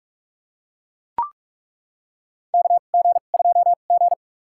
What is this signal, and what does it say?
Morse code, heard as a keyed tone, about 42 wpm, spaced as for 32 wpm, 700 Hz. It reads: KG2G
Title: USA Call Signs - 4 Characters - 2x Character Spacing - 45wpm